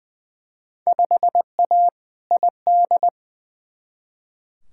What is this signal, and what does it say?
Morse code, heard as a keyed tone, about 20 wpm, 700 Hz. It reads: 5A ID
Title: Field Day Exchanges Encore - 20wpm